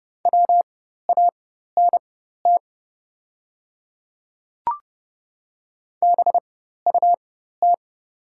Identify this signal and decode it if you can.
Morse code, heard as a keyed tone, about 30 wpm, spaced as for 14 wpm, 700 Hz. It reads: WANT BUT
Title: 4x Character Spacing - Top 500 Words - 30wpm